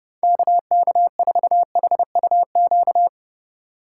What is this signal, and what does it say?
Morse code, heard as a keyed tone, about 31 wpm, 700 Hz. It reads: KK4HUQ
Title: USA Call Signs - 6 Characters - 3x Character Spacing - 30wpm